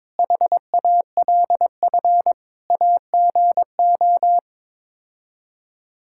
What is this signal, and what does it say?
Morse code, heard as a keyed tone, about 22 wpm, 700 Hz. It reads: HALF AGO
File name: practice / 22wpm / Sets of 7 Words - 22wpm